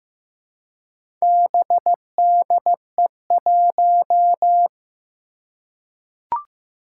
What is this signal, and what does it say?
Morse code, heard as a keyed tone, about 15 wpm, 700 Hz. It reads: BDE1
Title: Four Challenging Letter-Number Combinations 15wpm